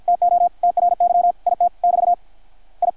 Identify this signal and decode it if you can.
Morse code, heard as a keyed tone, about 52 wpm, 660 Hz, with radio noise. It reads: TO TRY IT <BT> I
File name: archived / CRSnet